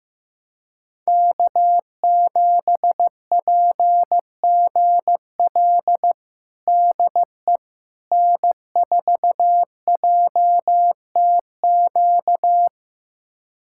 Morse code, heard as a keyed tone, about 15 wpm, 700 Hz. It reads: K7PGL DE N4JTQ
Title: USA Callsign DE Callsign - 5 Characters - 15wpm